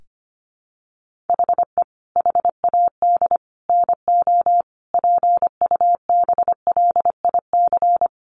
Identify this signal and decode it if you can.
Morse code, heard as a keyed tone, about 25 wpm, 700 Hz. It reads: HE HAD NO PUBLIC
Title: Sentences from Top 1500 Words 25wpm